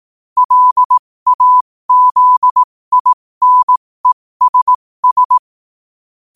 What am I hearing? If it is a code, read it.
Morse code, heard as a keyed tone, about 18 wpm, spaced as for 15 wpm, 1000 Hz. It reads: LAZINESS
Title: 15 WPM morse code quotes for Thu, 29 Feb 2024